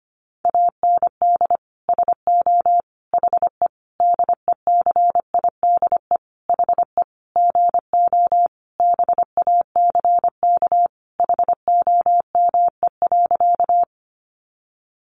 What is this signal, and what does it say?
Morse code, heard as a keyed tone, about 25 wpm, 700 Hz. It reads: AND SO HE DECIDE HE GO BACK HOME.
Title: Sentences from Top 500 Words 25wpm